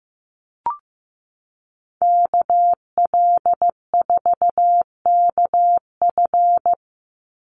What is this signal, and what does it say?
Morse code, heard as a keyed tone, about 15 wpm, 700 Hz. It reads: KL4KF
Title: USA Call Signs 15wpm